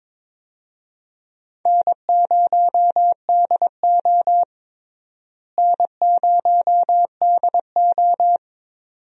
Morse code, heard as a keyed tone, about 21 wpm, 700 Hz. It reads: N0DO N0DO
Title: USA Call Signs - 4 Characters - Repeated 2x - 22wpm